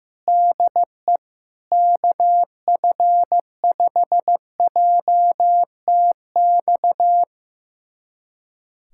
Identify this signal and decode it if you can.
Morse code, heard as a keyed tone, about 15 wpm, 700 Hz. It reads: DE KF5JTX
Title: USA Callsign DE Callsign - 6 Characters - 15wpm